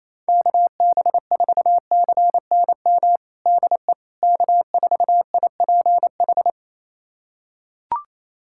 Morse code, heard as a keyed tone, about 28 wpm, 700 Hz. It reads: KB4CNM DE K4IPH
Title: USA Callsign DE Callsign 28wpm